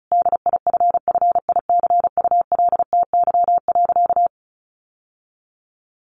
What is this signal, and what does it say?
Morse code, heard as a keyed tone, about 34 wpm, 700 Hz. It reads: DIFFICULTY.
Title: Sentences from Top 1500 Words 35wpm